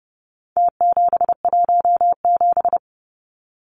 Morse code, heard as a keyed tone, about 30 wpm, 700 Hz. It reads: T717
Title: Numbers - Contest Style - 4-Digit 30wpm